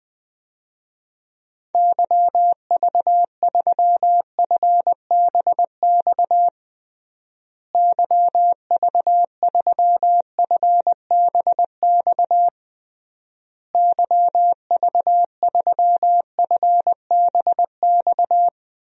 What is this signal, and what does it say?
Morse code, heard as a keyed tone, about 21 wpm, 700 Hz. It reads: YV3FBX YV3FBX YV3FBX
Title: DXCC-100 Call Signs - 6 Characters - Repeated 3x - 20wpm